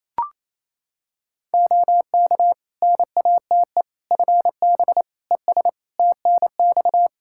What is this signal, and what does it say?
Morse code, heard as a keyed tone, about 28 wpm, 700 Hz. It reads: OK NATE FB ES TNX
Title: Practice QSO - Exchange 2 - Info - 28wpm